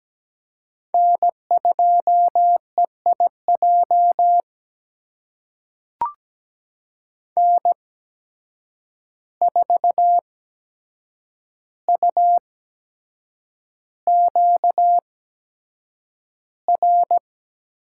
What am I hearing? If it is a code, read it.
Morse code, heard as a keyed tone, about 17 wpm, spaced as for 7 wpm, 700 Hz. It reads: N2EIJ N4UQR
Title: USA Call Signs - 5 Characters - 8x Character Spacing - 17wpm